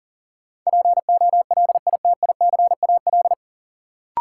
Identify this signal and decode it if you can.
Morse code, heard as a keyed tone, about 38 wpm, 700 Hz. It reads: POLITICAL
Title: Sentences from Top 800 Words - New Word Drill - 40wpm